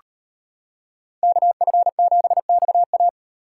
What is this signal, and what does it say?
Morse code, heard as a keyed tone, about 38 wpm, 700 Hz. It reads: KF7XA